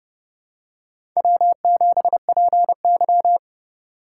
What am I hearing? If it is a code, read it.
Morse code, heard as a keyed tone, about 28 wpm, 690 Hz. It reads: W7PY
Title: USA Call Signs - 4 Characters - 2x Character Spacing - 30wpm